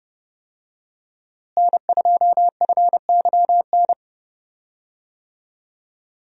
Morse code, heard as a keyed tone, about 28 wpm, 700 Hz. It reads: N2FYN
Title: USA Call Signs - 5 Characters - Repeated 2x - 30wpm